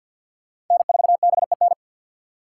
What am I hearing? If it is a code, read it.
Morse code, heard as a keyed tone, about 53 wpm, 700 Hz. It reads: N4BEN
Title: USA Call Signs - 5 Characters - 4x Character Spacing - 50wpm